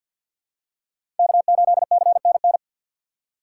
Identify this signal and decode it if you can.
Morse code, heard as a keyed tone, about 52 wpm, 700 Hz. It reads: K8CNN
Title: USA Call Signs - 5 Characters - 6x Character Spacing - 50wpm